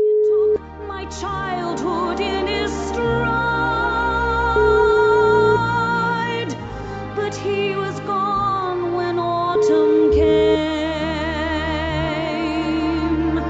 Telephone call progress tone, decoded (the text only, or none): ringback tone